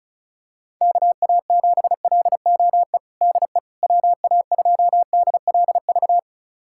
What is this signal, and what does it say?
Morse code, heard as a keyed tone, about 34 wpm, 700 Hz. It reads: KA7LOE DE WA2DLV
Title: USA Callsign DE Callsign - 6 Characters - 35wpm